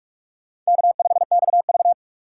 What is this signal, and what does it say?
Morse code, heard as a keyed tone, about 43 wpm, 700 Hz. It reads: K5XV
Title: USA Call Signs - 4 Characters - 3x Character Spacing - 45wpm